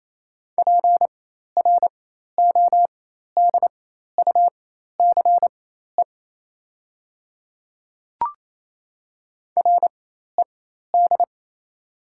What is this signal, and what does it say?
Morse code, heard as a keyed tone, about 28 wpm, spaced as for 13 wpm, 700 Hz. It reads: PRODUCE RED
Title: 4x Character Spacing - Top 500 Words - 28wpm